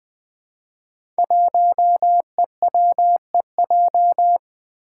Morse code, heard as a keyed tone, about 20 wpm, 700 Hz. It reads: 1EWEJ
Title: Five Challenging Letter-Number Combinations 20wpm